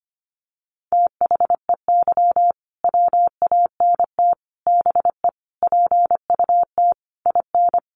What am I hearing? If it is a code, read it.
Morse code, heard as a keyed tone, about 25 wpm, 700 Hz. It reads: THEY WANT BE PUT IN
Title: Sentences from Top 1000 Words 25wpm